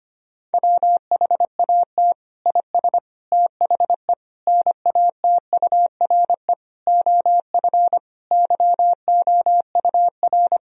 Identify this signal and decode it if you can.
Morse code, heard as a keyed tone, about 25 wpm, 700 Hz. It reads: WHAT IS THE NATURE OF YOUR
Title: Sentences from Top 1000 Words - Review Part 4 - 25wpm